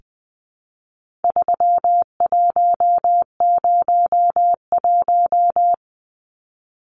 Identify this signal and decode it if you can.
Morse code, heard as a keyed tone, about 20 wpm, 700 Hz. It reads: 3101
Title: Numbers - 4-Digit 20wpm